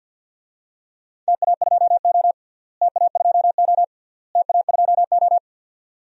Morse code, heard as a keyed tone, about 53 wpm, 700 Hz. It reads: TA2O TA2O TA2O
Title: DXCC-100 Call Signs - 4 Characters - Repeated 3x - 50wpm